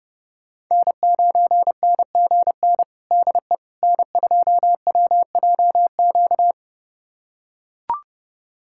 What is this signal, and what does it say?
Morse code, heard as a keyed tone, about 32 wpm, 700 Hz. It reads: N9NGN DE N2WJQ
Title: USA Callsign DE Callsign 30wpm